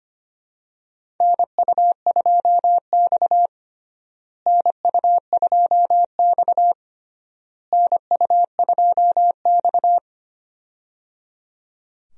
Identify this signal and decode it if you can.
Morse code, heard as a keyed tone, about 25 wpm, 700 Hz. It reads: NU2X NU2X NU2X
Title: USA Call Signs - 4 Characters - Repeated 3x - 25wpm